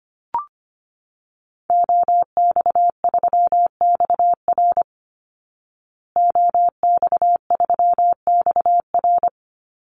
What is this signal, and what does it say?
Morse code, heard as a keyed tone, about 25 wpm, 700 Hz. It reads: OX3XR OX3XR
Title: Call Signs Repeated 3x - 25wpm